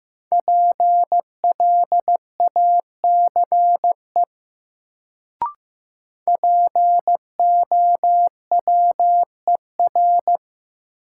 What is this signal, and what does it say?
Morse code, heard as a keyed tone, about 16 wpm, 700 Hz. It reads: PLACE POWER
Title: Sets of 4 Words - New Word Drill - 15wpm